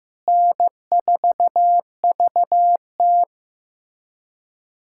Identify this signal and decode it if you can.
Morse code, heard as a keyed tone, about 15 wpm, 700 Hz. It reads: N4VT
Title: USA Callsign DE Callsign - 4 Characters - 15wpm